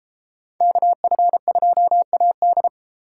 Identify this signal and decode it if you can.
Morse code, heard as a keyed tone, about 31 wpm, 700 Hz. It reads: KF2AD